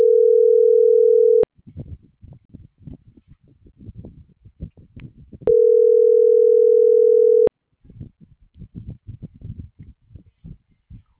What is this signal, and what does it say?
Ringback tone (North America), 440 + 480 Hz.